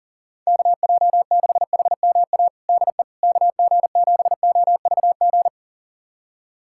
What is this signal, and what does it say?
Morse code, heard as a keyed tone, about 40 wpm, 700 Hz. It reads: KJ6HMA DE KG7OVG
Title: USA Callsign DE Callsign - 6 Characters - 40wpm